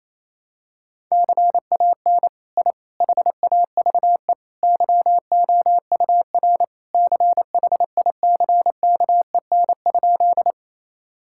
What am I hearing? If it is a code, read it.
Morse code, heard as a keyed tone, about 27 wpm, 700 Hz. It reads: CAN I HAVE YOUR CHICKEN?
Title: Sentences from Top 1000 Words - Review Part 5 - 28wpm